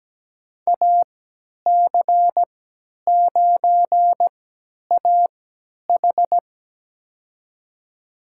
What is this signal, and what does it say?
Morse code, heard as a keyed tone, about 17 wpm, spaced as for 10 wpm, 700 Hz. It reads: AC9AH
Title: USA Call Signs - 5 Characters - 3x Character Spacing - 17wpm